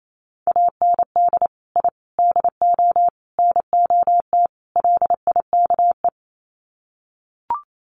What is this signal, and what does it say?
Morse code, heard as a keyed tone, about 28 wpm, 700 Hz. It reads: AND I DO NOT LIKE
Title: Sets of 5 Words Encore2 28wpm